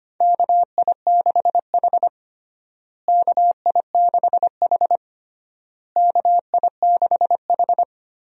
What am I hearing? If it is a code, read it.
Morse code, heard as a keyed tone, about 25 wpm, 700 Hz. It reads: KI6H KI6H KI6H
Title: USA Call Signs - 4 Characters - Repeated 3x - 25wpm